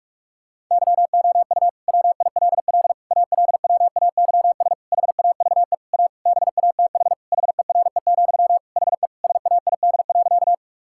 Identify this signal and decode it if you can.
Morse code, heard as a keyed tone, about 42 wpm, 700 Hz. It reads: YOU WILL ALWAYS HAVE A BATH HERE, HE SAID.
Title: Sentences from Top 1600 Words 45wpm